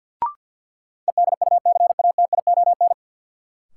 Morse code, heard as a keyed tone, about 51 wpm, 700 Hz. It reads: EDUCATION
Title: Sentences from Top 1100 Words - New Word Drill Alt - 50wpm